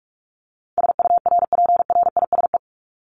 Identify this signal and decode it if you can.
Morse code, heard as a keyed tone, about 44 wpm, 700 Hz. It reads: SURPRISE
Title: Top 1751-2000 Words 45wpm